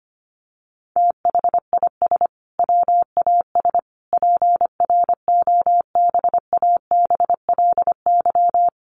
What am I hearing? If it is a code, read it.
Morse code, heard as a keyed tone, about 25 wpm, 700 Hz. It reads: THIS WAS PROBABLY
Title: Sentences from Top 700 Words 25wpm